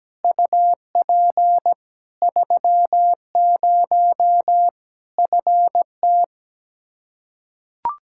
Morse code, heard as a keyed tone, about 17 wpm, 700 Hz. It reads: UP 30 FT
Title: Practice QSO - Exchange 2 - Info - 17wpm